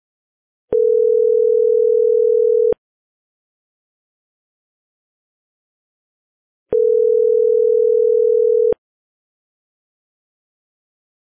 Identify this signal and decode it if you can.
Ringback tone (North America), 440 + 480 Hz.